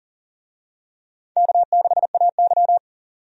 Morse code, heard as a keyed tone, about 38 wpm, 700 Hz. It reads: K6AY
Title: USA Call Signs - 4 Characters - 40wpm